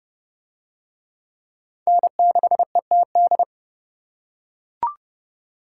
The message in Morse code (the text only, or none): N6ETD